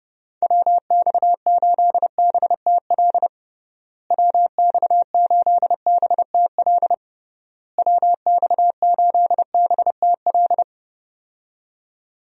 Morse code, heard as a keyed tone, about 30 wpm, 700 Hz. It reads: WX8BTL WX8BTL WX8BTL
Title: USA Call Signs - 6 Characters - Repeated 3x - 30wpm